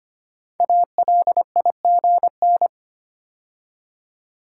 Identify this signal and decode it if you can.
Morse code, heard as a keyed tone, about 25 wpm, 700 Hz. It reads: ALIGN
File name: 5 Letter Words - First 3 Letters Given - 25wpm